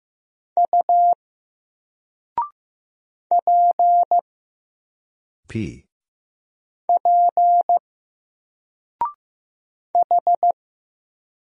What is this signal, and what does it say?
Morse code, heard as a keyed tone, about 15 wpm, 700 Hz. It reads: U P P H
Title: Single Letter-Number 15wpm